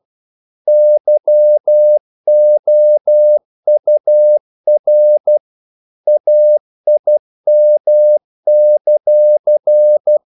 Morse code, heard as a keyed tone, about 12 wpm, 600 Hz. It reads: YOUR AIM;
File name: CW / Practice / If12